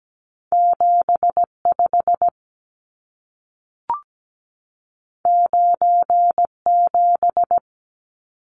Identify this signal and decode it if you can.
Morse code, heard as a keyed tone, about 17 wpm, 700 Hz. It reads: 75 97
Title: Numbers - 2-Digit 17wpm